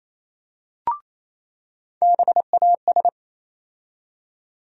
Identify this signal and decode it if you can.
Morse code, heard as a keyed tone, about 28 wpm, 700 Hz. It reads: BAS
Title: USA Call Sign Postfixes - 3 Letters 28wpm